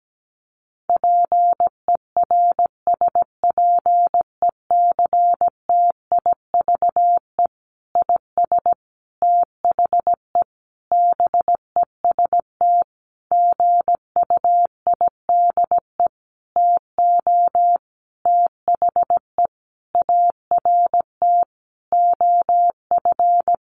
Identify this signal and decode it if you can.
Morse code, heard as a keyed tone, about 17 wpm, 700 Hz. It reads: PERSPECTIVE IS THE BEST GUIDE TO THE ART OF
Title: Sentences from Top 1200 Words 17wpm